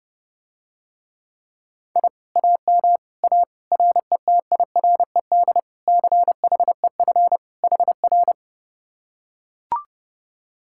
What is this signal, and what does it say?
Morse code, heard as a keyed tone, about 31 wpm, 700 Hz. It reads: I AM A RETIRED CHEF HR